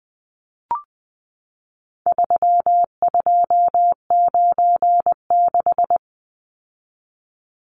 Morse code, heard as a keyed tone, about 20 wpm, 700 Hz. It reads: 3296
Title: Numbers - 4-Digit 20wpm